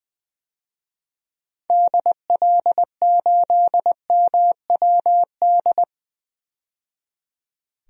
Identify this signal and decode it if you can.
Morse code, heard as a keyed tone, about 20 wpm, 700 Hz. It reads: DL8MWD
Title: DXCC-100 Call Signs - 6 Characters - 20wpm